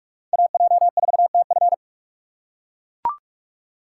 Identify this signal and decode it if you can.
Morse code, heard as a keyed tone, about 42 wpm, 700 Hz. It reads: AJ4TF
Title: K1USN SST QSO Practice - 45wpm